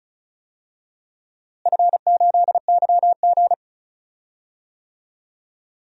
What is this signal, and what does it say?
Morse code, heard as a keyed tone, about 36 wpm, 700 Hz. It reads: F8YG
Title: DXCC-100 Call Signs - 4 Characters - Repeated 2x - 35wpm